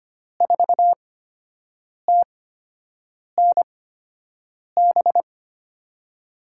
Morse code, heard as a keyed tone, about 25 wpm, spaced as for 7 wpm, 700 Hz. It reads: 4TNB
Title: USA Call Signs - 6 Characters - 8x Character Spacing - 25wpm